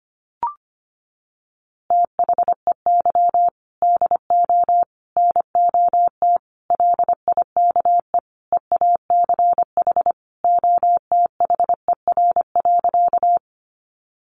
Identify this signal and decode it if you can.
Morse code, heard as a keyed tone, about 25 wpm, 700 Hz. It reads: THEY DO NOT LIKE EACH OTHER.
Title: Easy Phrases 25wpm